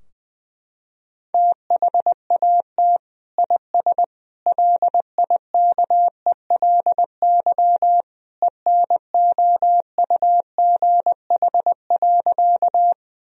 Morse code, heard as a keyed tone, about 19 wpm, 700 Hz. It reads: THAT IS LIKELY ENOUGH.
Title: Sentences from Top 1000 Words - Review Part 4 - 20wpm